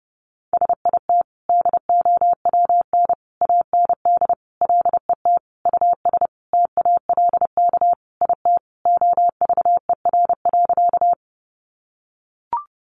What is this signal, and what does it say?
Morse code, heard as a keyed tone, about 30 wpm, 700 Hz. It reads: SIT DOWN AND LET US TALK IT OVER.
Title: Sentences from Top 500 Words 30wpm